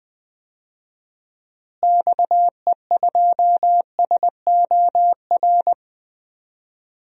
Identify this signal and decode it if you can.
Morse code, heard as a keyed tone, about 20 wpm, 700 Hz. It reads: XE2SOR
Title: DXCC-100 Call Signs - 6 Characters - Repeated 2x - 20wpm